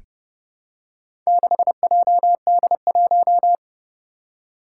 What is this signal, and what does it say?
Morse code, heard as a keyed tone, about 30 wpm, 700 Hz. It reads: 6JD1
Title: Four Challenging Letter-Number Combinations 30wpm